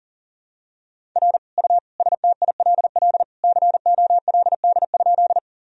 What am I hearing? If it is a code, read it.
Morse code, heard as a keyed tone, about 40 wpm, 700 Hz. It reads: R U STILL COLD?
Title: Sentences from Top 400 Words and Abbreviations 40wpm